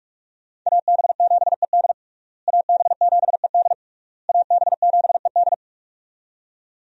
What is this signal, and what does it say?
Morse code, heard as a keyed tone, about 42 wpm, 700 Hz. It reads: AB7ED AB7ED AB7ED
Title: USA Call Signs - 5 Characters - Repeated 3x - 45wpm